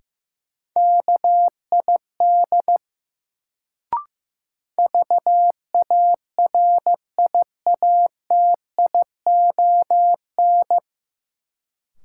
Morse code, heard as a keyed tone, about 15 wpm, 700 Hz. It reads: KID VARIATION
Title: Sentences from Top 1600 Words - New Word Drill Alt - 15wpm